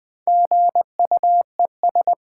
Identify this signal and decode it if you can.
Morse code, heard as a keyed tone, about 20 wpm, 700 Hz. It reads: GUES
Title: Build A Word - 1 Letter At A Time - 5-Letter Words - 20wpm